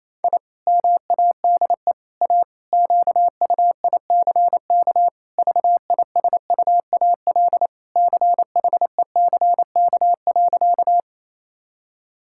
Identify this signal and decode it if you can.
Morse code, heard as a keyed tone, about 28 wpm, 700 Hz. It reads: I MADE A QUICK VISUAL CHECK.
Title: Sentences from Top 1000 Words - Review Part 6 - 28wpm